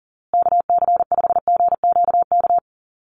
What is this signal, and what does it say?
Morse code, heard as a keyed tone, about 40 wpm, 700 Hz. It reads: KC5GQK
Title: USA Call Signs Repeated 2x 40wpm